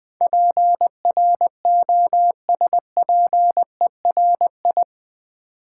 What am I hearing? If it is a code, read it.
Morse code, heard as a keyed tone, about 20 wpm, 700 Hz. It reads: PROSPERI
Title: Build A Word - 4 Letters At A Time - 10-Letter Words - 20wpm